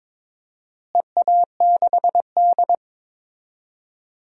Morse code, heard as a keyed tone, about 22 wpm, 700 Hz. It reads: EA6D
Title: DXCC-100 Call Signs - 4 Characters - Repeated 3x - 22wpm